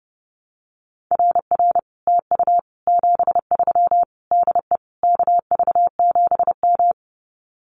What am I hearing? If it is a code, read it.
Morse code, heard as a keyed tone, about 30 wpm, 700 Hz. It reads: RR TU 73 DE KV7M
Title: Common QSO Phrases Encore 30wpm